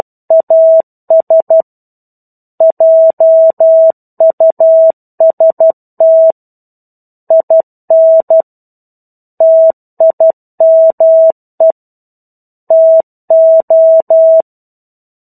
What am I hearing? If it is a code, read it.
Morse code, heard as a keyed tone, about 12 wpm, 650 Hz. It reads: AS JUST IN TIME TO
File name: 01-Chapter-01 12 WPM